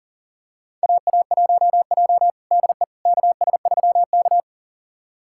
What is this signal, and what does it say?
Morse code, heard as a keyed tone, about 38 wpm, 700 Hz. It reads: AA1J DE KS3K
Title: USA Callsign DE Callsign - 4 Characters - 40wpm